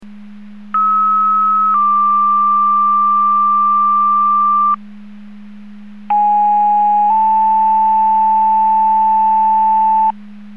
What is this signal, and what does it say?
Two-tone page, heard as a one-second tone, then a three-second tone, about 1290 then 1180 Hz, about 830 then 880 Hz. It is heard from 0.7 to 10.1 s.
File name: Portage 154.385 Paging